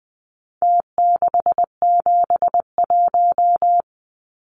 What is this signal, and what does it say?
Morse code, heard as a keyed tone, about 19 wpm, 700 Hz. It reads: T671
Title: Numbers - Contest Style - 4-Digit 20wpm